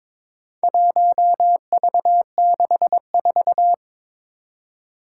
Morse code, heard as a keyed tone, about 22 wpm, 700 Hz. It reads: 1V64
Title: Four Challenging Letter-Number Combinations 22wpm